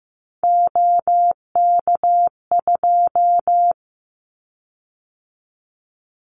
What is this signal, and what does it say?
Morse code, heard as a keyed tone, about 15 wpm, 700 Hz. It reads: OK2
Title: Callsign Prefixes for Top 100 DXCC - 15wpm